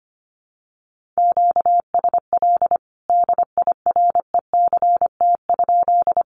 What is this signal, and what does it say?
Morse code, heard as a keyed tone, about 25 wpm, 700 Hz. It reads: QSL DIRECT?
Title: Common QSO Phrases 25wpm